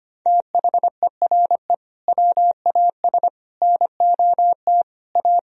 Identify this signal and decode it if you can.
Morse code, heard as a keyed tone, about 25 wpm, 700 Hz. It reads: THERE WAS NOT A
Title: Sentences from Top 1000 Words - Review Part 1 - 25wpm